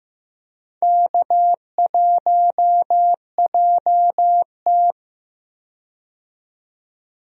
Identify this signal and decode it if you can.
Morse code, heard as a keyed tone, about 15 wpm, 700 Hz. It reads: K1JT
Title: USA Call Signs - 4 Characters - 15wpm